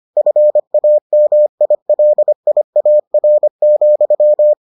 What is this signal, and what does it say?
Morse code, heard as a keyed tone, about 25 wpm, 600 Hz. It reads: FAMILIAR,